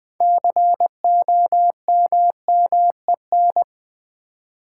Morse code, heard as a keyed tone, about 20 wpm, 700 Hz. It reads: COMMEN
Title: Build A Word - 2 Letters At A Time - 8-Letter Words - 20wpm